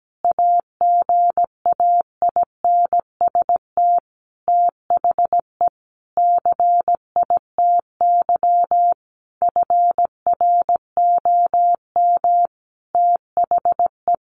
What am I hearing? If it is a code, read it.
Morse code, heard as a keyed tone, about 17 wpm, 700 Hz. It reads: AGAINST THE CITY FROM THE
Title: Sentences from Top 1200 Words 17wpm